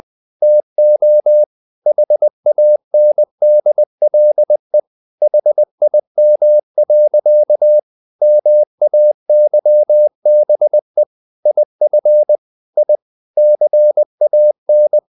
Morse code, heard as a keyed tone, about 20 wpm, 600 Hz. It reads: TO HANDLE HIM. MAYBE IF I CAN
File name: the old man and the sea / the old man and the sea part 1 20wpm